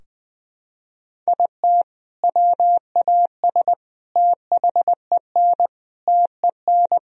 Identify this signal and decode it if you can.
Morse code, heard as a keyed tone, about 20 wpm, 700 Hz. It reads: IT WAS THEN TEN
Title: Sentences from Top 1000 Words - Review Part 2 - 20wpm